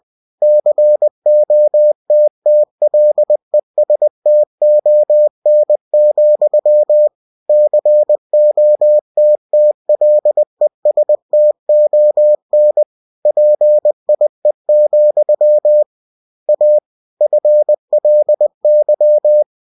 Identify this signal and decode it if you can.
Morse code, heard as a keyed tone, about 20 wpm, 600 Hz. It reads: COTTLESTON, COTTLESTON PIE, A FLY
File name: cw-books / winnie-the-pooh / winnie-20wpm-page-0045